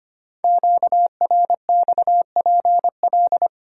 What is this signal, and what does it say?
Morse code, heard as a keyed tone, about 25 wpm, 700 Hz. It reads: QRXPL
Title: Code Groups - 5 Letters - 25wpm